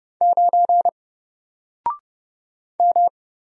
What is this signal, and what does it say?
Morse code, heard as a keyed tone, about 31 wpm, 700 Hz. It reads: 9 M
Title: Single Letter-Number Alt 30wpm